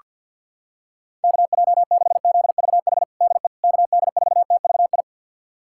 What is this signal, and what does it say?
Morse code, heard as a keyed tone, about 48 wpm, 700 Hz. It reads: KJ6ZVH DE KD4TVI
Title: USA Callsign DE Callsign 50wpm